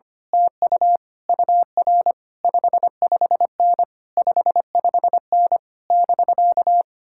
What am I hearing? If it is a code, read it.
Morse code, heard as a keyed tone, about 24 wpm, 700 Hz. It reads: TU UR 55N 55N <BK>
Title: Speed-Racing Common QSO Phrases - 25wpm